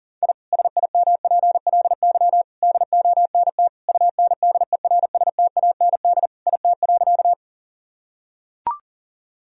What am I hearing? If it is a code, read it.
Morse code, heard as a keyed tone, about 40 wpm, 700 Hz. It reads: I SIMPLY DONT UNDERSTAND IT.
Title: Sentences from Top 1000 Words - Review Part 4 - 40wpm